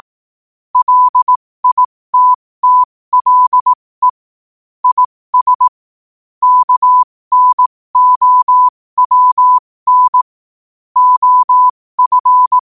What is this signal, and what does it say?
Morse code, heard as a keyed tone, about 18 wpm, spaced as for 15 wpm, 1000 Hz. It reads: LITTLE IS KNOWN OF
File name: qotdfiles / 15wpm